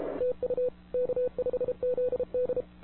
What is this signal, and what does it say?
Morse code, heard as a keyed tone, about 31 wpm, 500 Hz. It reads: TU K5ZD
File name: wwcw08 / k5zd wwcw08 1700z